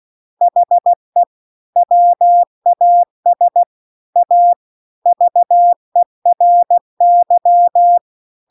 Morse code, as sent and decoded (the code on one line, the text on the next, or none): .... / .-- .- ... / .- / ...- . .-. -.--
HE WAS A VERY